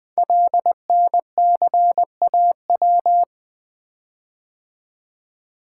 Morse code, heard as a keyed tone, about 20 wpm, 700 Hz. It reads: LNCAW
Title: Code Groups - 5 Letters - 20wpm